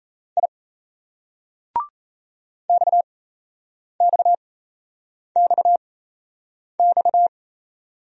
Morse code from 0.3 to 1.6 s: I

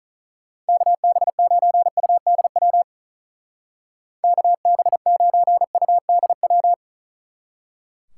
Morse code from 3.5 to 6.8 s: KB9UDW

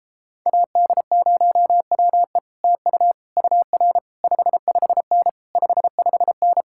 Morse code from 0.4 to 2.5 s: AD0WE

